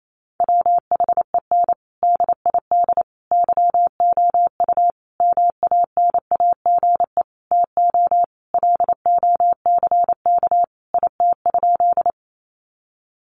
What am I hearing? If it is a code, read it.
Morse code, heard as a keyed tone, about 28 wpm, 700 Hz. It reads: WHEN DID YOU MANAGE TO LOCK IT?
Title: Sentences from Top 1400 Words 28wpm